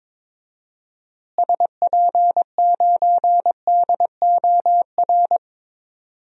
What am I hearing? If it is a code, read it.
Morse code, heard as a keyed tone, about 22 wpm, 700 Hz. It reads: SP9DOR
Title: DXCC-100 Call Signs - 6 Characters - Repeated 3x - 22wpm